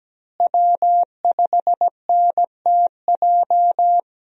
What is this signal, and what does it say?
Morse code, heard as a keyed tone, about 17 wpm, 700 Hz. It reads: W5NTJ
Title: USA Call Signs - 5 Characters - Repeated 3x - 17wpm